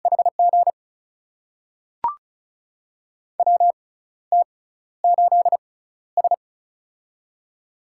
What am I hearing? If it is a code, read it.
Morse code, heard as a keyed tone, about 36 wpm, spaced as for 14 wpm, 700 Hz. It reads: HG WT8S
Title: USA Call Signs - 4 Characters - 6x Character Spacing - 35wpm